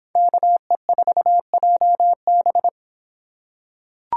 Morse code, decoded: KE4JB

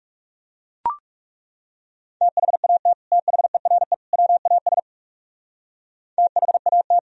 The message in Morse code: THAT THERE WAS THAT

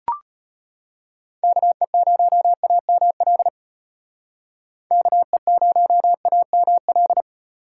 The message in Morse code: KE0AML KE0AML

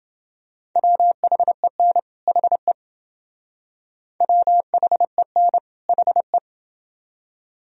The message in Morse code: WHEN HE WHEN HE

